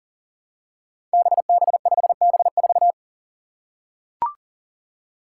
Morse code, heard as a keyed tone, about 38 wpm, 700 Hz. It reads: BB5B4
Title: Five Challenging Letter-Number Combinations 40wpm